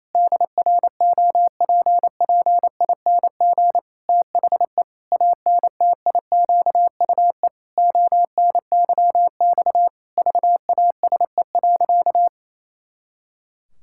Morse code, heard as a keyed tone, about 28 wpm, 700 Hz. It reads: DROPPING THE ANTIQUE ONYX VASE.
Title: Pangram Challenge v2 - 28wpm